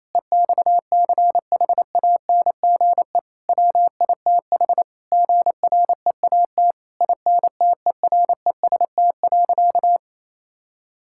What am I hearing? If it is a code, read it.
Morse code, heard as a keyed tone, about 28 wpm, 700 Hz. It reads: EXCHANGE WITH GREAT INTEREST.
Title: Sentences from Top 1000 Words - Review Part 6 - 28wpm